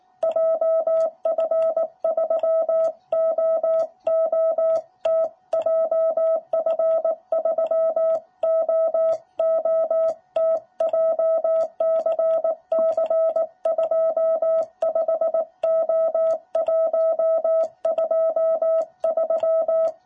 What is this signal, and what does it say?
Morse code, heard as a keyed tone, about 19 wpm, 660 Hz. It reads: JF3OOT JF3OOT JCC25O123
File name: jh8cbh / 2021cw